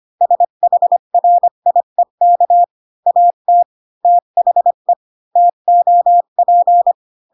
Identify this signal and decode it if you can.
Morse code, heard as a keyed tone, about 25 wpm, 700 Hz. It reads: SHRIEK AT THE TOP
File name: Pinocchio / Pinocchio ch5 20 25 ik1hge